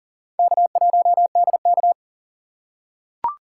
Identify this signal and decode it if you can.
Morse code, heard as a keyed tone, about 40 wpm, 700 Hz. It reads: K1DK
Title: USA Call Signs - 4 Characters - 4x Character Spacing - 40wpm